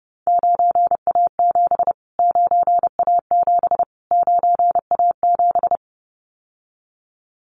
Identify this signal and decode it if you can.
Morse code, heard as a keyed tone, about 30 wpm, 690 Hz. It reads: 9A7 9A7 9A7
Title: Callsign Prefixes for Top 25 DXCC - Repeated 3x - 30wpm